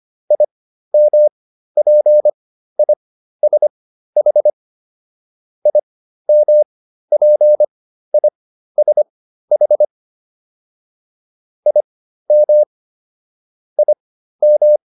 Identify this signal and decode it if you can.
Morse code, heard as a keyed tone, about 25 wpm, spaced as for 13 wpm, 600 Hz. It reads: IMPISH IMPISH IM IM